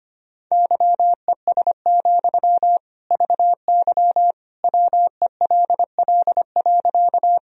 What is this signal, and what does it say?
Morse code, heard as a keyed tone, about 25 wpm, 700 Hz. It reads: YES, VY WELL.
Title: Sentences from Top 500 Words and Abbreviations 25wpm